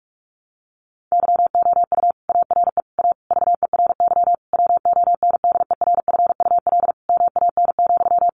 Morse code, heard as a keyed tone, about 46 wpm, 700 Hz. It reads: YOU ARE A VERY WONDERFUL MAN,
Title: Sentences from Top 1100 Words 45wpm